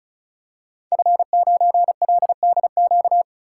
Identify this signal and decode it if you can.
Morse code, heard as a keyed tone, about 36 wpm, 700 Hz. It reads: F9LDQ
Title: Code Groups - 5 Letter:Numbers - 35wpm